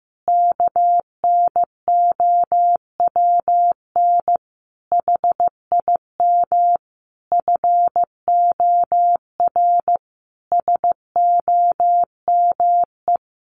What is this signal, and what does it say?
Morse code, heard as a keyed tone, about 14 wpm, 700 Hz. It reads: KNOWN HIM FOR SOME
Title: Sentences from Top 600 Words 15wpm